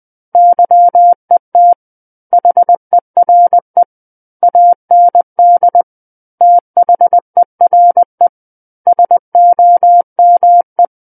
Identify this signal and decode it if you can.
Morse code, heard as a keyed tone, about 20 wpm, 700 Hz. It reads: YET HERE AND THERE SOME
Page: SKCC Learning Center - Morse Code Practice Files